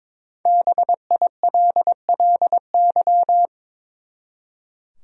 Morse code, heard as a keyed tone, about 21 wpm, 700 Hz. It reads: BILLY
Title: USA Common Names - 22wpm